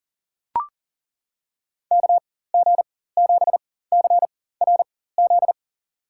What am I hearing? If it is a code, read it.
Morse code, heard as a keyed tone, about 40 wpm, spaced as for 19 wpm, 700 Hz. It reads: KG7CRZ
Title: USA Call Signs - 4x Character Spacing - 40wpm